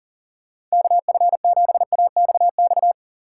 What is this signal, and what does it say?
Morse code, heard as a keyed tone, about 40 wpm, 700 Hz. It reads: KF7AXX
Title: USA Call Signs - 6 Characters - 2x Character Spacing - 40wpm